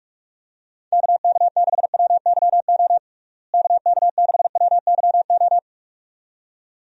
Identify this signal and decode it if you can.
Morse code, heard as a keyed tone, about 45 wpm, 700 Hz. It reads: KK6WYO KK6WYO
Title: USA Call Signs - 6 Characters - Repeated 2x - 45wpm